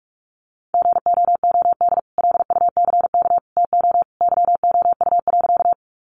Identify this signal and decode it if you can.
Morse code, heard as a keyed tone, about 44 wpm, 700 Hz. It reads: GOOD LUCK TO YOU.
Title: Sentences from Top 1500 Words 45wpm